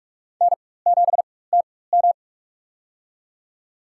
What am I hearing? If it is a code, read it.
Morse code, heard as a keyed tone, about 46 wpm, spaced as for 21 wpm, 700 Hz. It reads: N7TM
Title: USA Call Signs - 4x Character Spacing - 45wpm